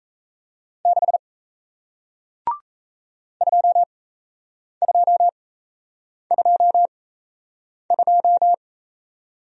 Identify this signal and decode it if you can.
Morse code, heard as a keyed tone, about 38 wpm, spaced as for 8 wpm, 700 Hz. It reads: 6 2222